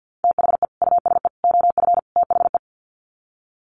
Morse code, heard as a keyed tone, about 50 wpm, 700 Hz. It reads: THE USE OF THE
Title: Sets of 4 Words 50wpm